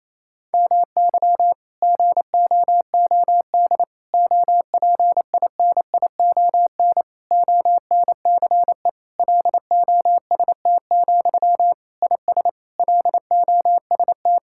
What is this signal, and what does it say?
Morse code, heard as a keyed tone, about 28 wpm, 700 Hz. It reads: MY GOOD OPINION ONCE LOST, IS LOST
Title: Sentences from Top 1600 Words 28wpm